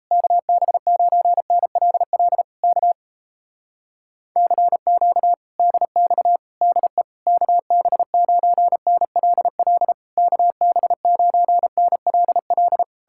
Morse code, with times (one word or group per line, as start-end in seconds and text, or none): 0.1-2.4 KB9NLL
2.6-2.9 K
4.4-5.3 CQ
5.6-6.4 DX
6.6-7.0 DE
7.3-9.9 KB9NLL
10.2-12.8 KB9NLL